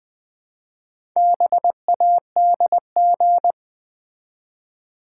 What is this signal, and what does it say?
Morse code, heard as a keyed tone, about 20 wpm, 700 Hz. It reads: BADG
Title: Build A Word - 1 Letter At A Time - 5-Letter Words - 20wpm